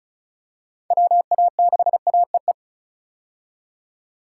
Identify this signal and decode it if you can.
Morse code, heard as a keyed tone, about 34 wpm, 700 Hz. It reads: WA6AEE
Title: K1USN SST QSO Practice - 35wpm